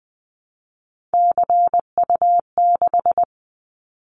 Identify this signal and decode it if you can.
Morse code, heard as a keyed tone, about 20 wpm, 700 Hz. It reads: CU6
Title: Callsign Prefixes for Top 100 DXCC - 20wpm